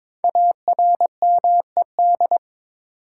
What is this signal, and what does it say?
Morse code, heard as a keyed tone, about 22 wpm, 700 Hz. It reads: ARMED
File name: 5 Letter Words - First 3 Letters Given - 22wpm